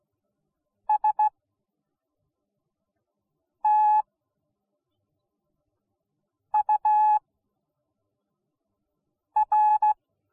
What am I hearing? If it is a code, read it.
Morse code, heard as a keyed tone, about 16 wpm, spaced as for 3 wpm, 850 Hz. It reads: STUR